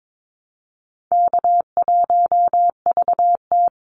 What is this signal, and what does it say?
Morse code, heard as a keyed tone, about 22 wpm, 700 Hz. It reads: K1VT
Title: Easy Call Signs Repeated 3x - 22wpm